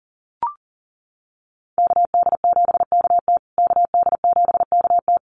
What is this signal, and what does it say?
Morse code, heard as a keyed tone, about 40 wpm, 700 Hz. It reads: KD7KT KD7KT
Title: USA Call Signs Repeated 2x 40wpm